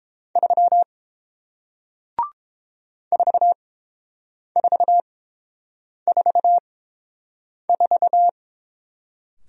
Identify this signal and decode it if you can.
Morse code, heard as a keyed tone, about 30 wpm, spaced as for 7 wpm, 700 Hz. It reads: 3 4444